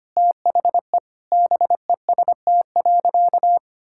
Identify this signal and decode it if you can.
Morse code, heard as a keyed tone, about 25 wpm, 700 Hz. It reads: THE BEST.
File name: practice / 25wpm / Sentences with 3-Word Phrases - 25wpm